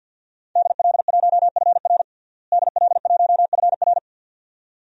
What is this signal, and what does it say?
Morse code, heard as a keyed tone, about 52 wpm, 700 Hz. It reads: DL1FR DL1FR
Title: DXCC-100 Call Signs - 5 Characters - Repeated 2x - 50wpm